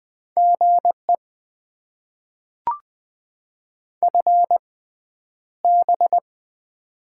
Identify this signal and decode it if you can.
Morse code, heard as a keyed tone, about 20 wpm, spaced as for 10 wpm, 690 Hz. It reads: GE FB
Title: Top 20 QSO Elements - 6x Character Spacing - 20wpm